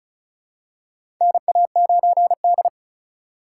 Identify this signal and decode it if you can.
Morse code, heard as a keyed tone, about 35 wpm, 700 Hz. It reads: NA9D
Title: USA Call Signs - 4 Characters - 4x Character Spacing - 35wpm